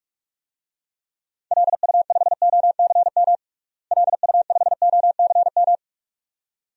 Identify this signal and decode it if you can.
Morse code, heard as a keyed tone, about 46 wpm, 690 Hz. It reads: LU5OCM LU5OCM
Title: DXCC-100 Call Signs - 6 Characters - Repeated 2x - 45wpm